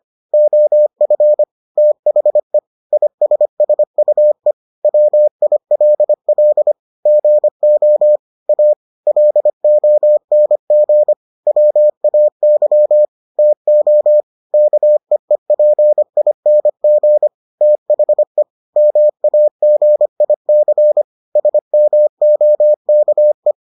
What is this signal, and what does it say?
Morse code, heard as a keyed tone, about 25 wpm, 600 Hz. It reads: OF THE ISSUE WILL GO A LONG WAY TO KEEPING THE MAGIC SMOKE